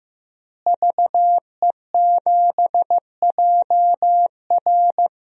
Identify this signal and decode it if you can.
Morse code, heard as a keyed tone, about 16 wpm, 700 Hz. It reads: VE7JR
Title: DXCC-100 Call Signs - 5 Characters - Repeated 2x - 15wpm